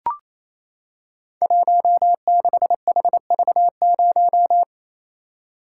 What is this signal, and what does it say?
Morse code, heard as a keyed tone, about 29 wpm, 700 Hz. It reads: 16HV0
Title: Code Groups - 5 Letter:Numbers - 28wpm